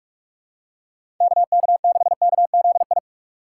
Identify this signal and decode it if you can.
Morse code, heard as a keyed tone, about 47 wpm, 700 Hz. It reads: KK6KZI
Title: USA Call Signs - 6 Characters - 4x Character Spacing - 45wpm